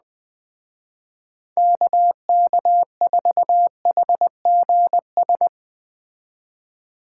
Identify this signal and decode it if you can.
Morse code, heard as a keyed tone, about 20 wpm, 700 Hz. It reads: KK4HGS